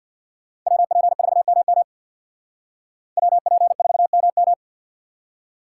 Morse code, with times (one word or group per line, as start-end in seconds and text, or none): 0.7-1.8 WP4MM
3.2-4.5 WP4MM